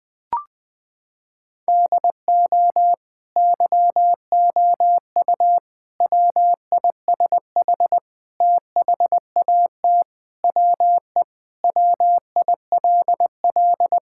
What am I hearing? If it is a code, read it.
Morse code, heard as a keyed tone, about 20 wpm, 700 Hz. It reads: DO YOU WISH THAT WE WILL
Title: Sentences from Top 1600 Words 20wpm